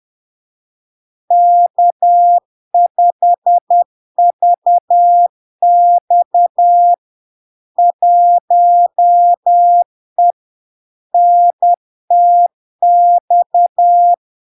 Morse code, heard as a keyed tone, about 10 wpm, 700 Hz. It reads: K5VX 1E NTX
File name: LARK-FD-2022-Log-in-CW-10wpm-700-Hz